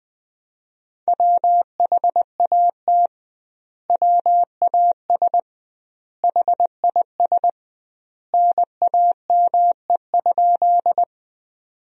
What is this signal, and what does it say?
Morse code, heard as a keyed tone, about 21 wpm, 700 Hz. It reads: WHAT WAS HIS NAME?
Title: 2x Word-Spacing - Sentences from Top 200 Words - 20wpm